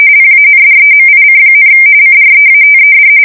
RTTY, about 45.45 baud. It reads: PANDEMIC, CREATED
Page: 170 Shift 2125/2295